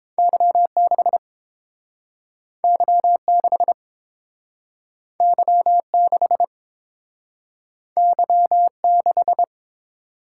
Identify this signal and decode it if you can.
Morse code, heard as a keyed tone, about 30 wpm, 700 Hz. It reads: Y6 Y6 Y6 Y6